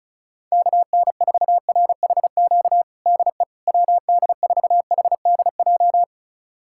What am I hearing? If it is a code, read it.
Morse code, heard as a keyed tone, about 34 wpm, 700 Hz. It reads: KN4RHQ DE WD4HDJ
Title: USA Callsign DE Callsign - 6 Characters - 35wpm